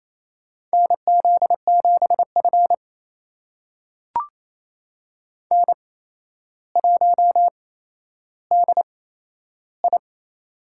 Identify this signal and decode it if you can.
Morse code, heard as a keyed tone, about 28 wpm, spaced as for 12 wpm, 700 Hz. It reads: NZ7F N1DI
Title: USA Call Signs - 4 Characters - 8x Character Spacing - 28wpm